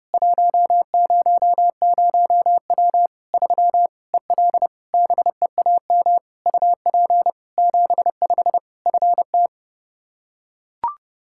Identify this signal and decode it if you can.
Morse code, heard as a keyed tone, about 30 wpm, 700 Hz. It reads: 100W 3 EL BEAM UP 75 FT
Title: K1USN SST QSO Practice 30wpm